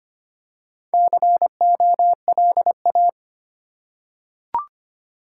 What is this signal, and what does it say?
Morse code, heard as a keyed tone, about 25 wpm, 700 Hz. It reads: COLA
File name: practice / 25wpm / Easy Words with Hints - 25wpm